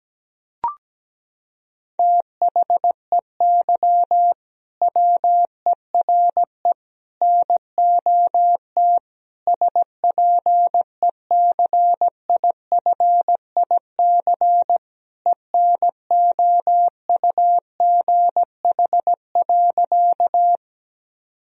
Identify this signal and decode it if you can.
Morse code, heard as a keyed tone, about 17 wpm, 700 Hz. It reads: THEY WERE NOT SPECIFIC ENOUGH.
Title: Sentences from Top 1000 Words - Review Part 3 - 17wpm